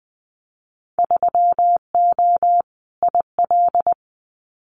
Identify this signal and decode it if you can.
Morse code, heard as a keyed tone, about 20 wpm, 700 Hz. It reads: 3O IL
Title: Winter Field Day Exchanges 20wpm